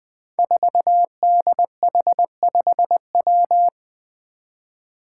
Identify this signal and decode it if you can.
Morse code, heard as a keyed tone, about 20 wpm, 700 Hz. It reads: 4DH5W
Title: Five Challenging Letter-Number Combinations 20wpm